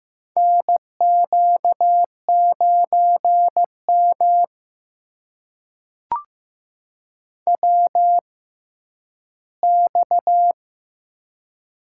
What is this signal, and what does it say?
Morse code, heard as a keyed tone, about 15 wpm, spaced as for 10 wpm, 700 Hz. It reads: NQ9M WX
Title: USA Call Signs - 4 Characters - 6x Character Spacing - 15wpm